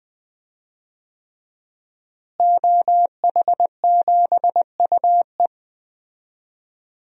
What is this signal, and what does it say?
Morse code, heard as a keyed tone, about 21 wpm, 700 Hz. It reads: OH7UE
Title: DXCC-100 Call Signs - 5 Characters - Repeated 3x - 20wpm